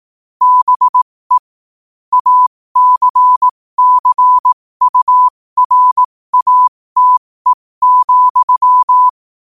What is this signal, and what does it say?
Morse code, heard as a keyed tone, about 18 wpm, spaced as for 15 wpm, 1000 Hz. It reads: BE ACCURATE,
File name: qotdfiles / 15wpm